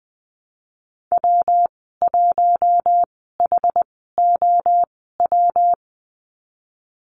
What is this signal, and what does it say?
Morse code, heard as a keyed tone, about 20 wpm, spaced as for 15 wpm, 700 Hz. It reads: W1HOW
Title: 128 Lesson 21g - Number 6 - Call Signs 20wpm